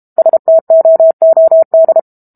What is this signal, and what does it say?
Morse code, heard as a keyed tone, about 32 wpm, 650 Hz. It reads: STOOD
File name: Book Two / 06-Chapter 6 - 32 WPM